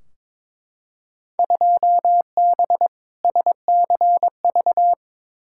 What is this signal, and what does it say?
Morse code, heard as a keyed tone, about 22 wpm, 700 Hz. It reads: 2B SCV
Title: Field Day Exchanges Encore - 22wpm